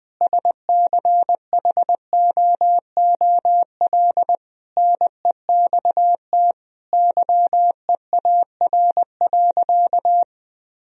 Morse code, heard as a keyed tone, about 20 wpm, 700 Hz. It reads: SCHOOL NEXT YEAR.
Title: Sentences from Top 1000 Words - Review Part 3 - 20wpm